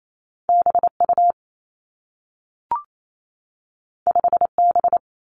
Morse code, heard as a keyed tone, about 28 wpm, 700 Hz. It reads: B U 5 B
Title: Two Challenging Letter-Number Combinations 28wpm